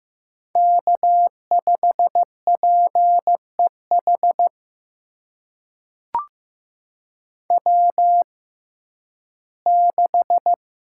Morse code, heard as a keyed tone, about 15 wpm, spaced as for 11 wpm, 700 Hz. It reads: K5PEH W6
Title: USA Call Signs - 5 Characters - 6x Character Spacing - 15wpm